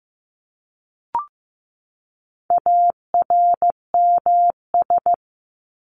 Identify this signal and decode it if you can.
Morse code, heard as a keyed tone, about 15 wpm, 700 Hz. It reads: ARMS
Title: ES:S Suffix - Top 100 Words with Base Words 15wpm